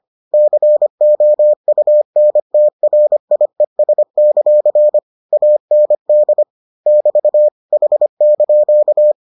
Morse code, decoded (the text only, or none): COUNTRIES; AND <BT> H)